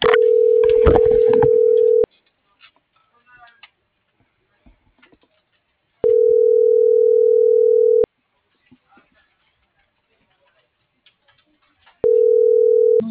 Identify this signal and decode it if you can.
Ringback tone (North America), 440 + 480 Hz.